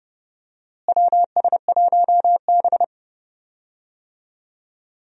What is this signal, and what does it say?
Morse code, heard as a keyed tone, about 32 wpm, 700 Hz. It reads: WS1B
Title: USA Call Signs - 4 Characters - Repeated 2x - 30wpm